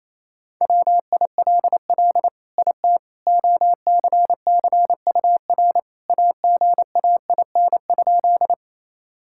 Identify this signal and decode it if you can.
Morse code, heard as a keyed tone, about 28 wpm, 700 Hz. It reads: WILL IT OCCUR AGAIN?
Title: Sentences from Top 1000 Words - Review Part 6 - 28wpm